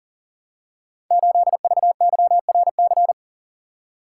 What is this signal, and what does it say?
Morse code, heard as a keyed tone, about 38 wpm, 700 Hz. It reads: 8VYRC
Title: Code Groups - 5 Letter:Numbers - 40wpm